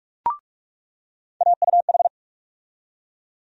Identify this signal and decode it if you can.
Morse code, heard as a keyed tone, about 42 wpm, 700 Hz. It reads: AUH
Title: USA Call Sign Postfixes - 3 Letters 45wpm